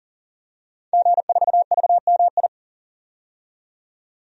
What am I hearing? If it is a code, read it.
Morse code, heard as a keyed tone, about 40 wpm, 700 Hz. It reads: G4VMI
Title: DXCC-100 Call Signs - 5 Characters - 40wpm